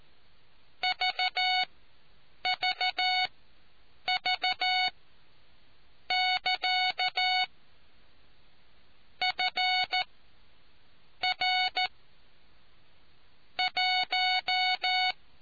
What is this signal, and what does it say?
Morse code, heard as a keyed tone, about 14 wpm, spaced as for 7 wpm, 740 Hz. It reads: VVV<KA> FR 1